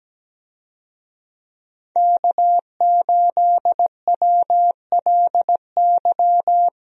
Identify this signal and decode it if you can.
Morse code, heard as a keyed tone, about 17 wpm, 700 Hz. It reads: K8WLY
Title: USA Call Signs - 5 Characters - 17wpm